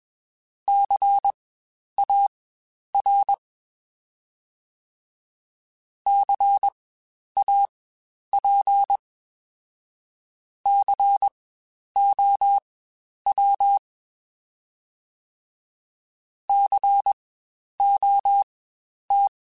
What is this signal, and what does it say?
Morse code, heard as a keyed tone, about 21 wpm, spaced as for 10 wpm, 800 Hz. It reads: CAR CAP COW COT
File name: Spaced Alphas / Spaced Alphas mp3s / Cs sp @20